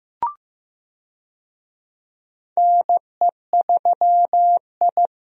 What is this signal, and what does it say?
Morse code, heard as a keyed tone, about 15 wpm, 700 Hz. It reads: NE3I
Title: USA Call Signs - 4 Characters - Repeated 3x - 15wpm